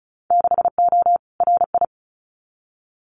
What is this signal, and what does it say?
Morse code, heard as a keyed tone, about 36 wpm, 700 Hz. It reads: 6O RI
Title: Winter Field Day Exchanges Encore 35wpm